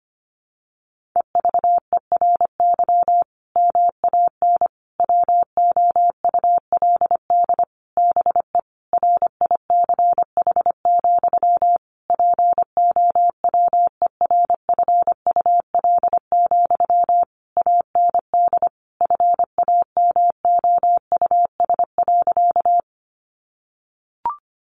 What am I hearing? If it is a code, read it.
Morse code, heard as a keyed tone, about 25 wpm, 700 Hz. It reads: EVERY MAN WOULD BE RICH, POWERFUL, AND FAMOUS.
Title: Sentences from Top 1500 Words 25wpm